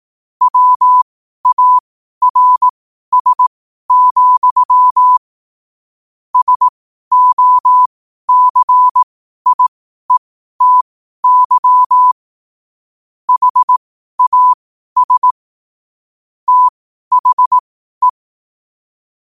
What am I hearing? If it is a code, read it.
Morse code, heard as a keyed tone, about 18 wpm, spaced as for 12 wpm, 1000 Hz. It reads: WARS, SOCIETY HAS THE